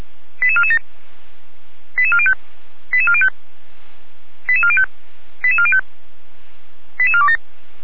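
Five-tone selective call (ZVEI), heard as quick runs of five tones, tones of about 70 ms. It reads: 88447 88405 88405 88405 88405 88427
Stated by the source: ZVEI five-tone selective call received by SDR, on the server of Feuerwehr Orlamünde (fire brigade).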